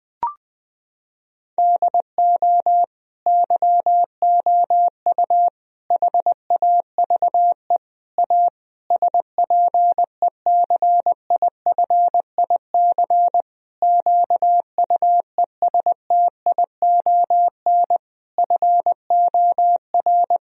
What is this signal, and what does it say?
Morse code, heard as a keyed tone, about 20 wpm, 700 Hz. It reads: DO YOU HAVE A SPECIFIC QUESTION FOR
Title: Sentences from Top 1000 Words - Review Part 3 - 20wpm